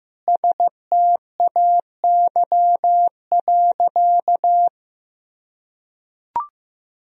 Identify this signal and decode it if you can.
Morse code, heard as a keyed tone, about 15 wpm, 700 Hz. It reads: STAY.
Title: Sentences from Top 1000 Words - Review Part 3 - 15wpm